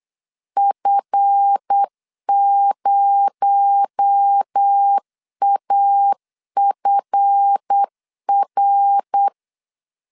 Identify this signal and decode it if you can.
Morse code, heard as a keyed tone, about 8 wpm, 800 Hz. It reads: F0AFR
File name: CW / DET / 1 9 1500C S C 8wpm 5653